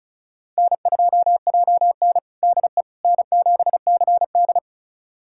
Morse code, heard as a keyed tone, about 36 wpm, 700 Hz. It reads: N2JN DE N7CD
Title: USA Callsign DE Callsign - 4 Characters - 35wpm